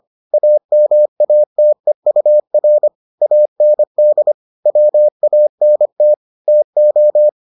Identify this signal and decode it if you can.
Morse code, heard as a keyed tone, about 25 wpm, 600 Hz. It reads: AMATEUR AND WANT TO